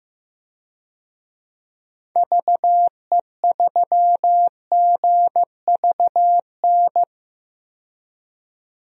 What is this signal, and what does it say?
Morse code, heard as a keyed tone, about 15 wpm, 700 Hz. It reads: VE3GVN
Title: DXCC-100 Call Signs - 6 Characters - 15wpm